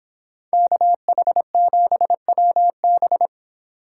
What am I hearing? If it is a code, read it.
Morse code, heard as a keyed tone, about 23 wpm, 700 Hz. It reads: KH7WB